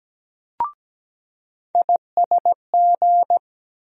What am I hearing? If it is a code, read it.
Morse code, heard as a keyed tone, about 17 wpm, 700 Hz. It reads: ISG
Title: USA Call Sign Postfixes - 3 Letters 17wpm